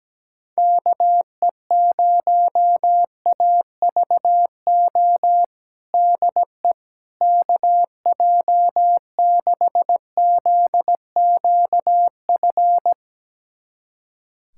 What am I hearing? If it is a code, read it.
Morse code, heard as a keyed tone, about 17 wpm, 700 Hz. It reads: KE0AVO DE KJ6ZQF
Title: USA Callsign DE Callsign 17wpm